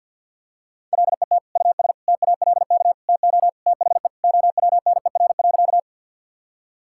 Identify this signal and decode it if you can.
Morse code, heard as a keyed tone, about 47 wpm, 700 Hz. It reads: LET US TALK TO THE OWNER.
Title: Sentences from Top 1600 Words 50wpm